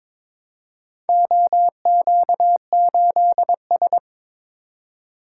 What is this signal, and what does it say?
Morse code, heard as a keyed tone, about 21 wpm, 700 Hz. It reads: OQ8S
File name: practice / 22wpm / Four Letter-Number Combinations - 22wpm